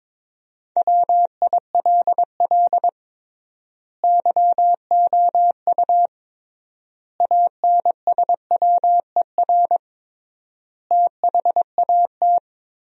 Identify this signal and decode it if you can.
Morse code, heard as a keyed tone, about 22 wpm, 700 Hz. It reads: WILL YOU ANSWER THAT
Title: 3x Word-Spacing - Sentences from Top 400 Words - 22wpm